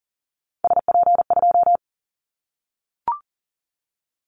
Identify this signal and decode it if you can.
Morse code, heard as a keyed tone, about 39 wpm, 700 Hz. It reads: SP2
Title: Callsign Prefixes for Top 25 DXCC - 40wpm